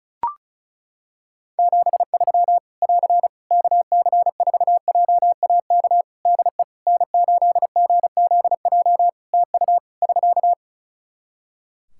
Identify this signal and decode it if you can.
Morse code, heard as a keyed tone, about 36 wpm, 700 Hz. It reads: 73 <AR> KC4JAK DE N8GZJ TU <SK>
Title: Practice QSO - Exchange 4 - Ending - 35wpm